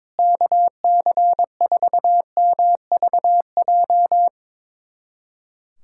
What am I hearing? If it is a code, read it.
Morse code, heard as a keyed tone, about 22 wpm, 700 Hz. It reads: KC4MVJ
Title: USA Callsign DE Callsign - 6 Characters - 22wpm